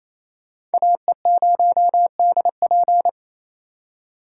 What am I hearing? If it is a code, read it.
Morse code, heard as a keyed tone, about 36 wpm, 700 Hz. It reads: AE0DP